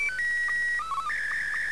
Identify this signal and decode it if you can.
SSTV (slow-scan television), Martin 2.